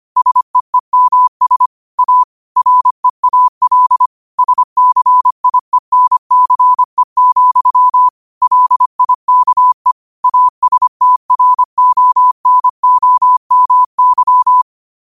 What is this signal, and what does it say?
Morse code, heard as a keyed tone, about 25 wpm, 1000 Hz. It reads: SEEMS A REAL SCIENCE, LIKE ASTRONOMY
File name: qotdfiles / 25wpm